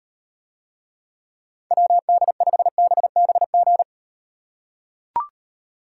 Morse code, heard as a keyed tone, about 37 wpm, 700 Hz. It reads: WD5BBG